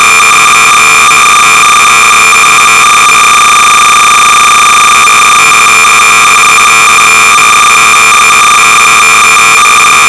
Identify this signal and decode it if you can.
Cassette data (Kansas City type).
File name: sorcerer / 300baud / Snake-(unknown author-modifed by Ian Gillies and called Smunch)-(v4)-BASIC-Sorcerer-300baud